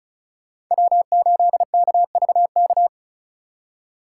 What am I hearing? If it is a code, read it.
Morse code, heard as a keyed tone, about 43 wpm, 700 Hz. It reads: W8KVK
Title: Speed Racing - USA 1x3 Call Signs 35wpm